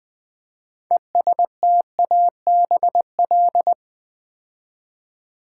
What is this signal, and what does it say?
Morse code, heard as a keyed tone, about 20 wpm, 700 Hz. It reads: ESTABL
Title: Build A Word - 3 Letters At A Time - 9-Letter Words - 20wpm